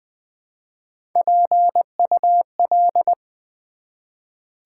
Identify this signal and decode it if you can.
Morse code, heard as a keyed tone, about 20 wpm, 700 Hz. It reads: PUL
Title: Build A Word - 1 Letter At A Time - 5-Letter Words - 20wpm